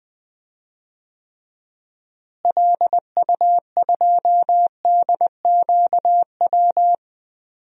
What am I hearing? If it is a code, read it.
Morse code, heard as a keyed tone, about 21 wpm, 700 Hz. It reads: LU2DQW
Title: DXCC-100 Call Signs - 6 Characters - Repeated 3x - 20wpm